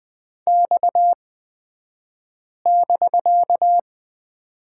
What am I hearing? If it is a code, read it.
Morse code, heard as a keyed tone, about 20 wpm, spaced as for 5 wpm, 700 Hz. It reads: X<BK>
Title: 249 Lesson 40d - BK - Two Characters 20wpm